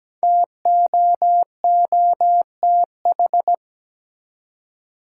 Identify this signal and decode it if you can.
Morse code, heard as a keyed tone, about 17 wpm, 700 Hz. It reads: TOOTH
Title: Five Letter Words - 17wpm